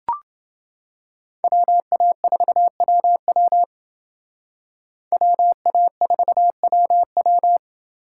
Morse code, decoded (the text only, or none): WA4WW WA4WW